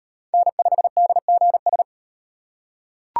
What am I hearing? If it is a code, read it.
Morse code, heard as a keyed tone, about 36 wpm, 700 Hz. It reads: N5DGS